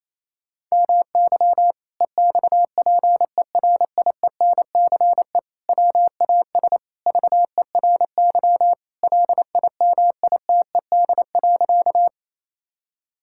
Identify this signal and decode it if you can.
Morse code, heard as a keyed tone, about 28 wpm, 700 Hz. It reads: MY EXPERIENCE WAS VERY LIMITED.
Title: Sentences from Top 1000 Words - Review Part 6 - 28wpm